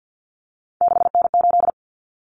Morse code, heard as a keyed tone, about 50 wpm, 700 Hz. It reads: 6N8
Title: Common Cut Numbers - 3-Digit 50wpm